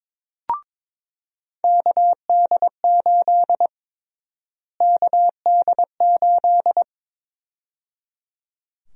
Morse code, heard as a keyed tone, about 22 wpm, 700 Hz. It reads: KD8 KD8
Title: USA Call Sign Prefixes - Repeated 2x - 22wpm